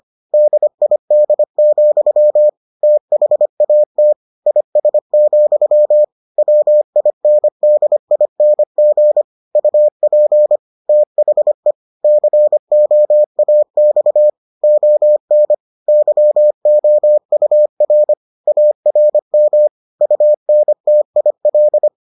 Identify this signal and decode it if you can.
Morse code, heard as a keyed tone, about 25 wpm, 600 Hz. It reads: DID, THAT IS, WINDING UP THE COAX ON YOUR ARM UNTIL